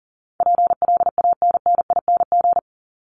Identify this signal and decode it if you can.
Morse code, heard as a keyed tone, about 40 wpm, 700 Hz. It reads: PLANNING
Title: ING Suffix - Top 100 Words with Base Words 40wpm